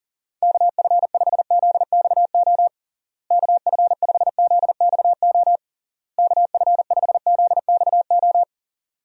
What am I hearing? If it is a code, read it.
Morse code, heard as a keyed tone, about 40 wpm, 700 Hz. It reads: KF5ZXO KF5ZXO KF5ZXO
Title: USA Call Signs - 6 Characters - Repeated 3x - 40wpm